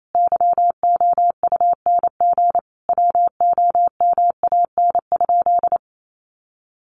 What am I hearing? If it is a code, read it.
Morse code, heard as a keyed tone, about 28 wpm, 700 Hz. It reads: YOUNG WOMAN?
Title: Sentences from Top 1100 Words 28wpm